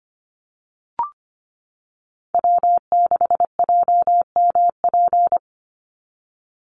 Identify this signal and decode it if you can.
Morse code, heard as a keyed tone, about 25 wpm, 700 Hz. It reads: W6JMP
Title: NA SOTA Chaser Callsigns 25wpm